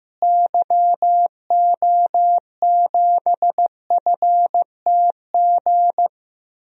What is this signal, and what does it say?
Morse code, heard as a keyed tone, about 15 wpm, 700 Hz. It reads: YO7FTG
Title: DXCC-100 Call Signs - 6 Characters - Repeated 3x - 15wpm